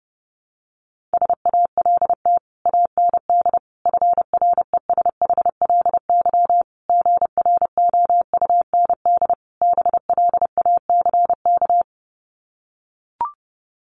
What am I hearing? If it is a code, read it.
Morse code, heard as a keyed tone, about 30 wpm, 700 Hz. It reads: SALT AND FRESHLY GROUND BLACK
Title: Sets of 5 Words Encore 30wpm